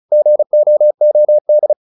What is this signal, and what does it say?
Morse code, heard as a keyed tone, about 34 wpm, 600 Hz. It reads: GOOD